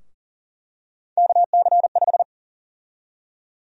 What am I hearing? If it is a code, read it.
Morse code, heard as a keyed tone, about 40 wpm, 700 Hz. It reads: KC5
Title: USA Call Sign Prefixes - 40wpm